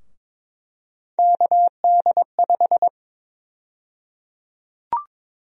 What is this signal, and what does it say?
Morse code, heard as a keyed tone, about 22 wpm, 700 Hz. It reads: KD5
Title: USA Call Sign Prefixes - 22wpm